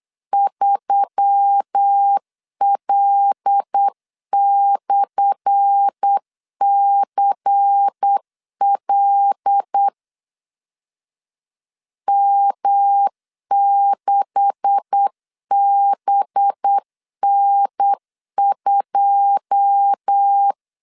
Morse code, heard as a keyed tone, about 8 wpm, 800 Hz. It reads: 3L/CL M6BN2
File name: LEZIONI / 10 SIGNS / 10C DOT 8wpm 3649